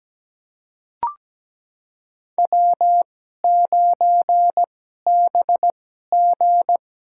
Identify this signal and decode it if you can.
Morse code, heard as a keyed tone, about 17 wpm, spaced as for 12 wpm, 700 Hz. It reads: W9BG
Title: USA Call Signs - 4 Characters - 2x Character Spacing - 17wpm